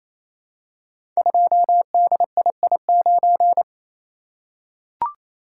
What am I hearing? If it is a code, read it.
Morse code, heard as a keyed tone, about 28 wpm, 700 Hz. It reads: 2DII9
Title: Code Groups - 5 Letter:Numbers - 28wpm